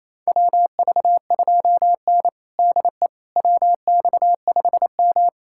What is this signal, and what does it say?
Morse code, heard as a keyed tone, about 28 wpm, 700 Hz. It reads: WV2N DE WX5M
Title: USA Callsign DE Callsign - 4 Characters - 28wpm